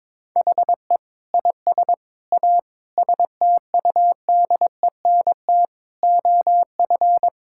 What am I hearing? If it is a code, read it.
Morse code, heard as a keyed tone, about 22 wpm, 700 Hz. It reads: HE IS A STUDENT OF
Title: Sentences from Top 1000 Words - Review Part 6 - 22wpm